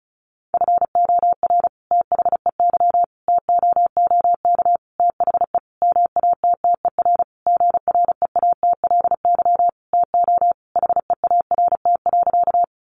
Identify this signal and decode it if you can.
Morse code, heard as a keyed tone, about 34 wpm, 700 Hz. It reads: FOR THEY TOOK THE MATTER GREATLY TO HEART.
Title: Sentences from Top 1400 Words 35wpm